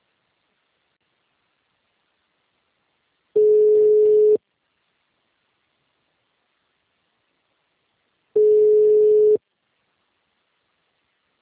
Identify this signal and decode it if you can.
Ringback tone (Europe, 425 Hz).